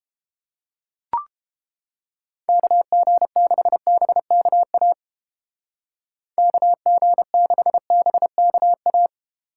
Morse code, heard as a keyed tone, about 32 wpm, 700 Hz. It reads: KG6BKA KG6BKA